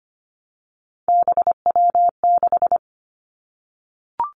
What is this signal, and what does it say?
Morse code, heard as a keyed tone, about 25 wpm, 700 Hz. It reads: BW6
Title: Three Challenging Letter-Number Combinations 25wpm